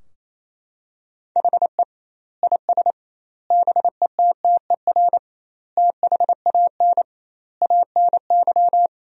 Morse code, heard as a keyed tone, about 28 wpm, 700 Hz. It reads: HE IS BETTER THAN ANY
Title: 2x Word-Spacing - Sentences from Top 400 Words - 28wpm